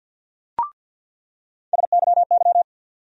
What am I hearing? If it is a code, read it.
Morse code, heard as a keyed tone, about 47 wpm, 700 Hz. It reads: SYY
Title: USA Call Sign Postfixes - 3 Letters 50wpm